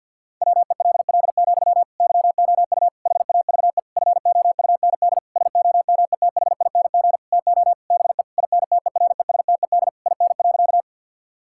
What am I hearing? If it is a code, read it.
Morse code, heard as a keyed tone, about 50 wpm, 700 Hz. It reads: WELL, YOU HAVE FOUND SOMETHING TO BE INTERESTED IN.
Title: Sentences from Top 1000 Words - Review Part 6 - 50wpm